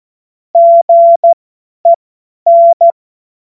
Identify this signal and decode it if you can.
Morse code, heard as a keyed tone, about 14 wpm, spaced as for 10 wpm, 680 Hz. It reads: GEN
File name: Morsekurs / Witze / Witz 12 Dezember 2023 50 F70